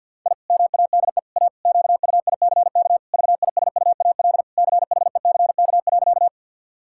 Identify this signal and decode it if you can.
Morse code, heard as a keyed tone, about 51 wpm, 700 Hz. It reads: I MADE A QUICK VISUAL CHECK.
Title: Sentences from Top 1000 Words - Review Part 6 - 50wpm